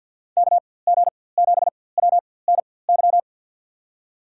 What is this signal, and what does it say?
Morse code, heard as a keyed tone, about 51 wpm, spaced as for 23 wpm, 700 Hz. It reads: KG7WNY
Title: USA Call Signs - 6 Characters - 4x Character Spacing - 50wpm